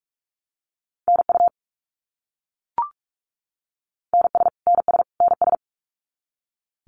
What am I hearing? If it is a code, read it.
Morse code, heard as a keyed tone, about 44 wpm, 700 Hz. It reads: NU NS NS NS
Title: Canadian Provinces and Territories - Repeated 3x - 45wpm